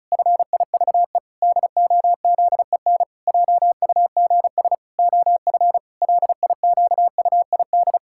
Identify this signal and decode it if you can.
Morse code, heard as a keyed tone, about 36 wpm, 700 Hz. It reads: FIVE DOZEN JUGS OF LIQUID
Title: Pangram Challenge v2 - 35wpm